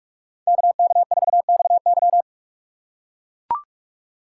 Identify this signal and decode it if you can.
Morse code, heard as a keyed tone, about 41 wpm, 700 Hz. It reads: KK4XY
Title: USA Call Signs - 5 Characters - 6x Character Spacing - 45wpm